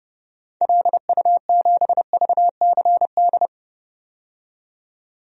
Morse code, heard as a keyed tone, about 30 wpm, 700 Hz. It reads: LU7VCD
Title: DXCC-100 Call Signs - 6 Characters - 30wpm